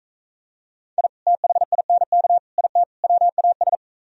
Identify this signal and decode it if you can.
Morse code, heard as a keyed tone, about 43 wpm, 700 Hz. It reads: I THINK IT WAS